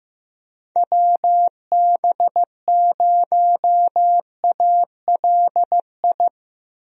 Morse code, heard as a keyed tone, about 15 wpm, 700 Hz. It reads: WB0ALI
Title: USA Call Signs - 6 Characters - Repeated 2x - 15wpm